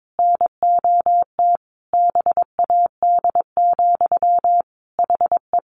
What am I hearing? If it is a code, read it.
Morse code, heard as a keyed tone, about 22 wpm, 700 Hz. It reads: NOT BAD, HE
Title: Sentences from Top 700 Words 22wpm